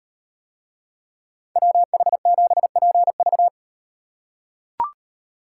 Morse code, heard as a keyed tone, about 36 wpm, 700 Hz. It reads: WH7PV